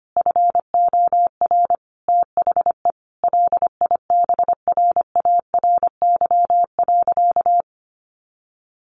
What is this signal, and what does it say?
Morse code, heard as a keyed tone, about 25 wpm, 700 Hz. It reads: FOR THE LIBRARY.
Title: Sentences from Top 1400 Words 25wpm